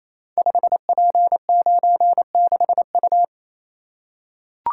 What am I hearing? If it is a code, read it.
Morse code, heard as a keyed tone, about 29 wpm, 700 Hz. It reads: 5P96U
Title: Code Groups - 5 Letter:Numbers - 28wpm